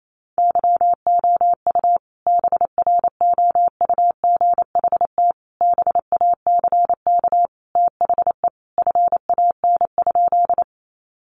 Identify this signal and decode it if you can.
Morse code, heard as a keyed tone, about 28 wpm, 700 Hz. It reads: YOU BROUGHT BACK THE FAN?
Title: Sentences from Top 1300 Words 28wpm